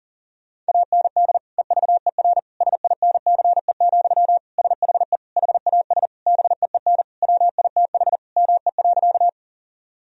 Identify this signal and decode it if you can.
Morse code, heard as a keyed tone, about 37 wpm, 700 Hz. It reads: AND EVER SINCE, SHE HAS BEEN WITH ME.
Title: Sentences from Top 1000 Words - Review Part 1 - 40wpm